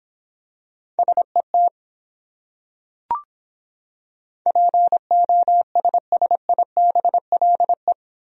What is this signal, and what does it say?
Morse code, heard as a keyed tone, about 26 wpm, 700 Hz. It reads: SET POSSIBLE